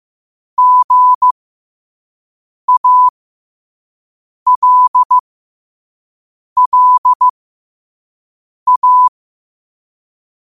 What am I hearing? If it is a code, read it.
Morse code, heard as a keyed tone, about 15 wpm, spaced as for 5 wpm, 1000 Hz. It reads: GALLA
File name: qotdfiles / 5wpm